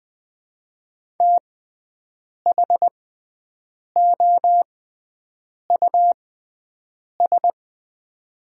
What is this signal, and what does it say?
Morse code, heard as a keyed tone, about 20 wpm, spaced as for 7 wpm, 700 Hz. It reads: THOUS
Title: 6x Character Spacing - Top 500 Words - 20wpm